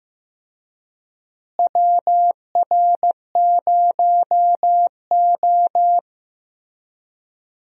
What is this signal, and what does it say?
Morse code, heard as a keyed tone, about 15 wpm, 700 Hz. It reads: WR0O
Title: USA Call Signs - 4 Characters - Repeated 3x - 15wpm